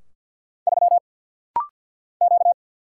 Morse code, heard as a keyed tone, about 50 wpm, 700 Hz. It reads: 3 Q
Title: Single Letter-Number - Rapid-Fire 50wpm